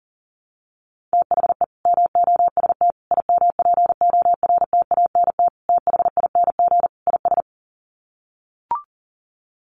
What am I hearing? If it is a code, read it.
Morse code, heard as a keyed tone, about 40 wpm, 700 Hz. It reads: THE MOST IMPORTANT THING IS
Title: Sets of 5 Words Encore 40wpm